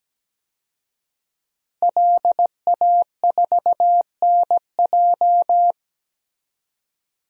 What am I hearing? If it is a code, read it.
Morse code, heard as a keyed tone, about 17 wpm, 700 Hz. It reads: LA4NJ
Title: DXCC-100 Call Signs - 5 Characters - Repeated 2x - 17wpm